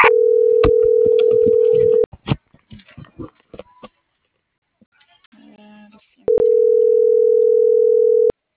Ringback tone (North America), 440 + 480 Hz.